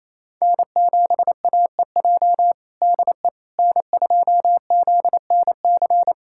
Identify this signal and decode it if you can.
Morse code, heard as a keyed tone, about 28 wpm, 700 Hz. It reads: N7AEJ DE N2ZNC
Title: USA Callsign DE Callsign 28wpm